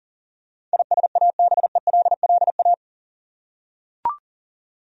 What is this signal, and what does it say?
Morse code, heard as a keyed tone, about 40 wpm, 700 Hz. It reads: ISABELLA
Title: USA Common Names - 40wpm